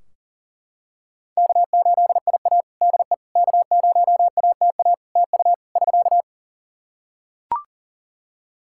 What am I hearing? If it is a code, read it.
Morse code, heard as a keyed tone, about 40 wpm, 700 Hz. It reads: K8IA DE K0ATA TU <SK>
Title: Practice QSO - Exchange 4 - Ending - 40wpm